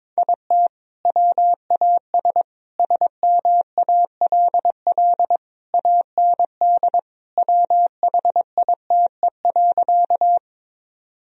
Morse code, heard as a keyed tone, about 22 wpm, 700 Hz. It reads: IT WAS SMALL AND WHITE.
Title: Sentences from Top 1000 Words - Review Part 1 - 22wpm